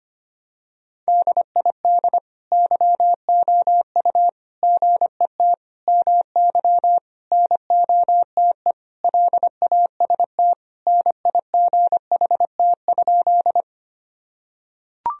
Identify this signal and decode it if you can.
Morse code, heard as a keyed tone, about 25 wpm, 700 Hz. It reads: DID YOU GET MY NOTE LAST NIGHT?
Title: Sentences from Top 1000 Words - Review Part 2 - 25wpm